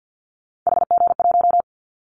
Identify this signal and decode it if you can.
Morse code, heard as a keyed tone, about 50 wpm, 700 Hz. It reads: HG1
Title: Callsign Prefixes for Top 25 DXCC - 50wpm